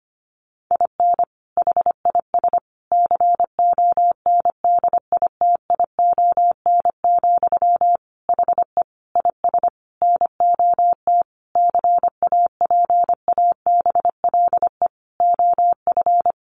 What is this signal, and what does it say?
Morse code, heard as a keyed tone, about 25 wpm, 700 Hz. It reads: IN HIS CONDITION, HE IS NOT CAPABLE OF
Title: Sentences from Top 1200 Words 25wpm